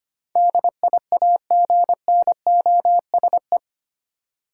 Morse code, heard as a keyed tone, about 25 wpm, 700 Hz. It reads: DIAGNOSE
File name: Eight Letter Words - 25wpm